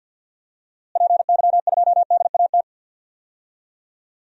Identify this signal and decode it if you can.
Morse code, heard as a keyed tone, about 52 wpm, 700 Hz. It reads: PY2DAT
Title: DXCC-100 Call Signs - 6 Characters - Repeated 3x - 50wpm